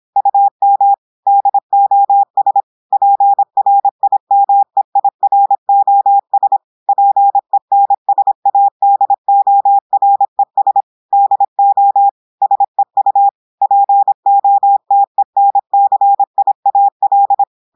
Morse code, heard as a keyed tone, about 26 wpm, 800 Hz. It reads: UM DOS PRIMEIROS PENSADORES DO SEU POTENCIAL
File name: audios / portugais / portugais texte en clair 1 vitesse 26 mots minutes